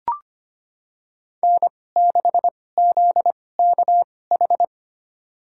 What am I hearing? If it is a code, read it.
Morse code, heard as a keyed tone, about 25 wpm, spaced as for 18 wpm, 700 Hz. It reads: N6ZKH
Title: USA Call Signs - 5 Characters - 2x Character Spacing - 25wpm